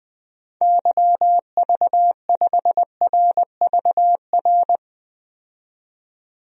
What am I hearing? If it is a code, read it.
Morse code, heard as a keyed tone, about 20 wpm, 700 Hz. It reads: YV5RVR
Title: DXCC-100 Call Signs - 6 Characters - 20wpm